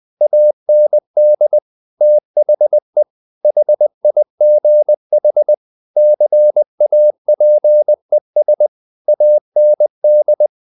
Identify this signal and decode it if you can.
Morse code, heard as a keyed tone, about 20 wpm, 600 Hz. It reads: AND THE HIGH CAPES AND
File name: the old man and the sea / the old man and the sea part 1 20wpm